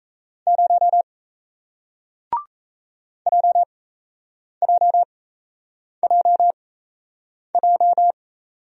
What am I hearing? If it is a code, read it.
Morse code, heard as a keyed tone, about 38 wpm, spaced as for 8 wpm, 700 Hz. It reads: 0 JJJJ